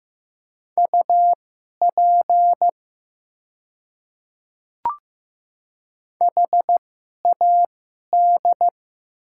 Morse code, heard as a keyed tone, about 15 wpm, spaced as for 11 wpm, 700 Hz. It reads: UP HAD
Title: 2x Character Spacing - Top 100 Words - 15wpm